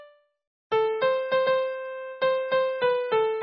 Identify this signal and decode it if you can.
There is no code.